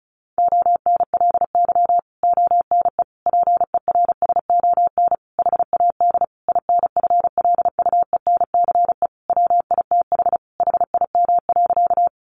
Morse code, heard as a keyed tone, about 34 wpm, 700 Hz. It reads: ONLY ONE PERSON HAD INFLUENCE WITH HIM.
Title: Sentences from Top 1100 Words 35wpm